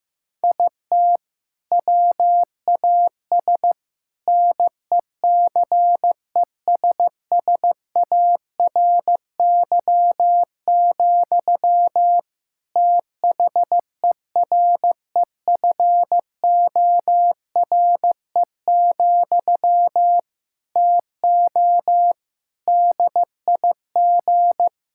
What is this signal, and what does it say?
Morse code, heard as a keyed tone, about 15 wpm, 700 Hz. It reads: IT WAS NECESSARY, THEREFORE, TO DIG
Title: Sentences from Top 1600 Words 15wpm